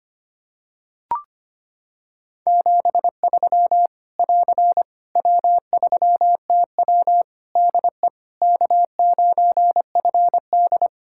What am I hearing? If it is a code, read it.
Morse code, heard as a keyed tone, about 25 wpm, 700 Hz. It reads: 73 <AR> W3TW DE K9FD
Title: Practice QSO - Exchange 4 - Ending - 25wpm